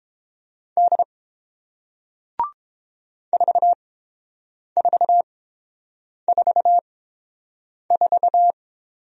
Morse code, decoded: D 4444

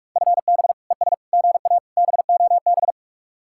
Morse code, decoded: FB ES GA BOB

